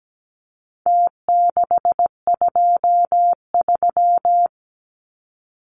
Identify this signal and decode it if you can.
Morse code, heard as a keyed tone, about 17 wpm, 700 Hz. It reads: T623
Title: Numbers - Contest Style - 4-Digit 17wpm